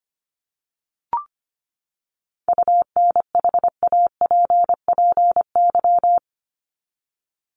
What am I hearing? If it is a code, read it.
Morse code, heard as a keyed tone, about 25 wpm, 700 Hz. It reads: UNHAPPY
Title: Top 1501-1750 Words 25wpm